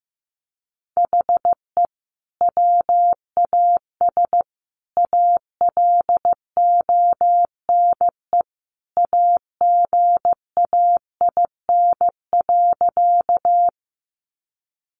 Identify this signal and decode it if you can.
Morse code, heard as a keyed tone, about 15 wpm, 700 Hz. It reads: HE WAS ALONE AGAIN.
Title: Sentences from Top 1000 Words 15wpm